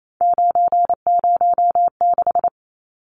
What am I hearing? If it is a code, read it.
Morse code, heard as a keyed tone, about 28 wpm, 700 Hz. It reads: 906
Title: Numbers - 3-Digit 28wpm